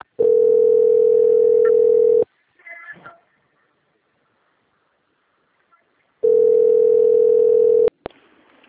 Ringback tone (North America), 440 + 480 Hz.